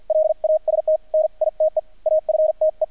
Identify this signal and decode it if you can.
Morse code, heard as a keyed tone, about 48 wpm, spaced as for 41 wpm, 620 Hz, with radio noise. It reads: LAST NITE AFTE
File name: archived / CRSnet